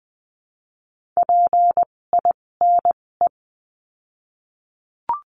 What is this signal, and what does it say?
Morse code, heard as a keyed tone, about 20 wpm, spaced as for 16 wpm, 700 Hz. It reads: PINE
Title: 256 Lesson 42a - Words 12-20wpm